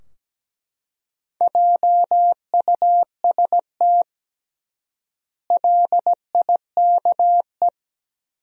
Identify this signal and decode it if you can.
Morse code, heard as a keyed tone, about 17 wpm, 700 Hz. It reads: JUST LIKE
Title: 3x Word-Spacing - Sentences from Top 200 Words - 17wpm